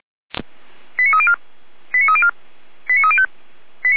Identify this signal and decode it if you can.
Five-tone selective call (ZVEI), heard as quick runs of five tones, tones of about 70 ms. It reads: 89304 89304 89305